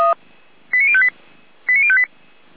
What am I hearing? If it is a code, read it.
Five-tone selective call (ZVEI), heard as quick runs of five tones, tones of about 70 ms. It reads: 89058 89058
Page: SDR/ZVEI